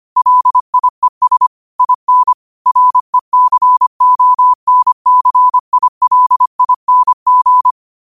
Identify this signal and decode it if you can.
Morse code, heard as a keyed tone, about 25 wpm, 1000 Hz. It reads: LIES IN RECONCILING
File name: qotdfiles / 25wpm